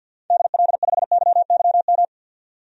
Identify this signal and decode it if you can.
Morse code, heard as a keyed tone, about 46 wpm, 700 Hz. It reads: DL5YYM
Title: DXCC-100 Call Signs - 6 Characters - 50wpm